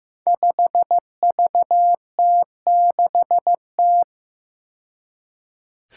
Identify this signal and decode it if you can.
Morse code, heard as a keyed tone, about 15 wpm, 700 Hz. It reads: 5VT6T
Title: Five Challenging Letter-Number Combinations 15wpm